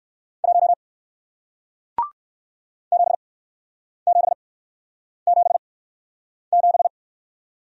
Morse code, heard as a keyed tone, about 60 wpm, spaced as for 9 wpm, 700 Hz. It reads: 1 7777